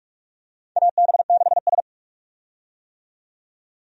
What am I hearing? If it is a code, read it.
Morse code, heard as a keyed tone, about 43 wpm, 700 Hz. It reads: AB6S
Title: USA Call Signs - 4 Characters - 45wpm